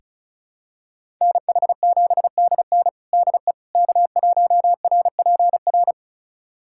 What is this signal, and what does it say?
Morse code, heard as a keyed tone, about 36 wpm, 700 Hz. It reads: NH7DN DE K1RPR
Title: USA Callsign DE Callsign 35wpm